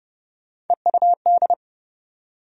Morse code, heard as a keyed tone, about 32 wpm, 700 Hz. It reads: EUD
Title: USA Call Sign Postfixes - 3 Letters 30wpm